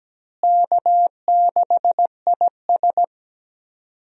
Morse code, heard as a keyed tone, about 17 wpm, 700 Hz. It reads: K6IS
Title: USA Call Signs - 4 Characters - Repeated 3x - 17wpm